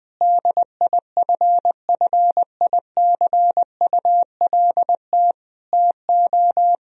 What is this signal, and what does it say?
Morse code, heard as a keyed tone, about 21 wpm, 700 Hz. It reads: DIFFICULT TO
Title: Sentences from Top 1000 Words - Review Part 4 - 20wpm